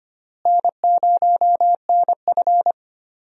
Morse code, heard as a keyed tone, about 25 wpm, 700 Hz. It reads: N0NF
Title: USA Call Signs - 4 Characters - Repeated 3x - 25wpm